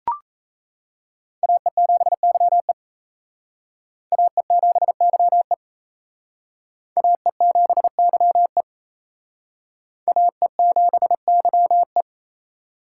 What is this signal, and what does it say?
Morse code, heard as a keyed tone, about 35 wpm, 700 Hz. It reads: AE7YE AE7YE AE7YE AE7YE
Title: Speed Racing - USA 2x2 Call Signs 28wpm